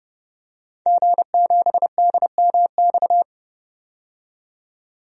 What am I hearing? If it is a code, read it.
Morse code, heard as a keyed tone, about 30 wpm, 700 Hz. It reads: G7DMX
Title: DXCC-100 Call Signs - 5 Characters - 30wpm